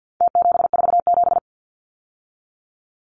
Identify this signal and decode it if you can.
Morse code, heard as a keyed tone, about 49 wpm, 700 Hz. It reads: T747
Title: Numbers - Contest Style - 4-Digit 50wpm